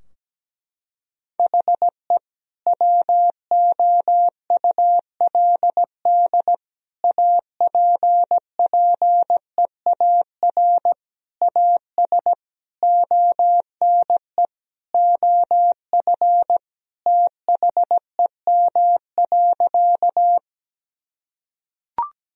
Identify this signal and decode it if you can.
Morse code, heard as a keyed tone, about 17 wpm, 700 Hz. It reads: HE WOULD APPEAR AS ONE OF THEM.
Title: Sentences from Top 1000 Words - Review Part 2 - 17wpm